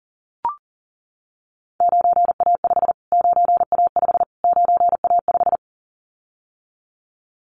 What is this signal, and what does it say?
Morse code, heard as a keyed tone, about 40 wpm, 700 Hz. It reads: 9A5 9A5 9A5
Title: Callsign Prefixes for Top 25 DXCC - Repeated 3x - 40wpm